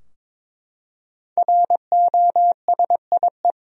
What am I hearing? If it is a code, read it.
Morse code, heard as a keyed tone, about 22 wpm, 700 Hz. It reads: ROSIE
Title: USA Common Names - 22wpm